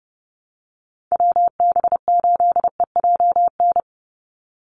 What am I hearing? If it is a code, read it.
Morse code, heard as a keyed tone, about 30 wpm, 700 Hz. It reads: WB8EJN
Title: Call Signs Repeated 2x - 30wpm